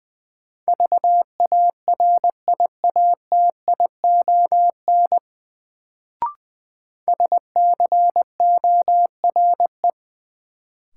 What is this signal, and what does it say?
Morse code, heard as a keyed tone, about 20 wpm, 700 Hz. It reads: VARIATION SCORE
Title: Sentences from Top 1600 Words - New Word Drill - 20wpm